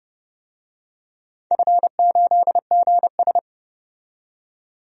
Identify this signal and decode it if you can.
Morse code, heard as a keyed tone, about 30 wpm, 700 Hz. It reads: F8GS
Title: DXCC-100 Call Signs - 4 Characters - 30wpm